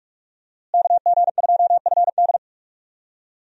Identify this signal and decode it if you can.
Morse code, heard as a keyed tone, about 45 wpm, 700 Hz. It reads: KG2FD
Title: USA Call Signs - 5 Characters - Repeated 2x - 45wpm